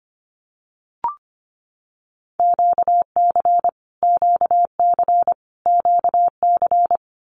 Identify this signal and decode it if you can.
Morse code, heard as a keyed tone, about 25 wpm, 700 Hz. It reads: QC QC QC
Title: Canadian Provinces and Territories - Repeated 3x - 25wpm